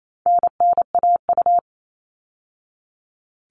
Morse code, heard as a keyed tone, about 28 wpm, 700 Hz. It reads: NNAU
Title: All Cut Numbers - 4-Digit 28wpm